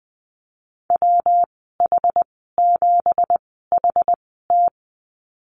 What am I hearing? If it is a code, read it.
Morse code, heard as a keyed tone, about 20 wpm, 700 Hz. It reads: W H 7 H T
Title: 207 Lesson 33g - Letter j - Call Signs 20wpm